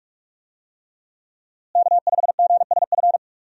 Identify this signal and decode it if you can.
Morse code, heard as a keyed tone, about 41 wpm, 700 Hz. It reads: K5GSF
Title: USA Call Signs - 5 Characters - Repeated 3x - 45wpm